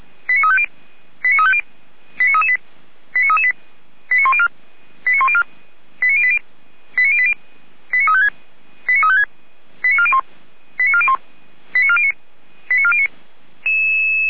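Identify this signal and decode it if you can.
Five-tone selective call (ZVEI), heard as quick runs of five tones, tones of about 70 ms. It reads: 89350 89350 89338 89338 89104 89104 89080 89080 89356 89356 89401 89401 89409 89409
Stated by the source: SDR reception of ZVEI five-tone call 89164, from the feuerwehr-orlamuende.org fire brigade server.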